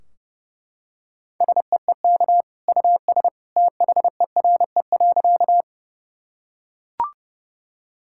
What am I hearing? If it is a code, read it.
Morse code, heard as a keyed tone, about 30 wpm, 700 Hz. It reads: SEEK US THERE.
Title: Sentences from Top 1600 Words 30wpm